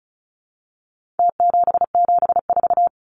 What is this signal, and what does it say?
Morse code, heard as a keyed tone, about 34 wpm, 700 Hz. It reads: T774
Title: Numbers - Contest Style - 4-Digit 35wpm